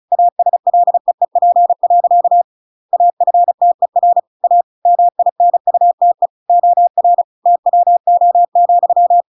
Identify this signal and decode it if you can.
Morse code, heard as a keyed tone, about 34 wpm, 700 Hz. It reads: ASLEEP. AFTER A MINUTE OR TWO,